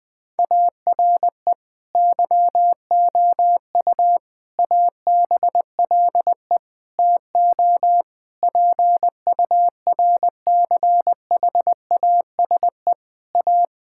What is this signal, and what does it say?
Morse code, heard as a keyed tone, about 21 wpm, 700 Hz. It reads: ARE YOU ABLE TO PURCHASE A
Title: Sentences from Top 1000 Words - Review Part 5 - 20wpm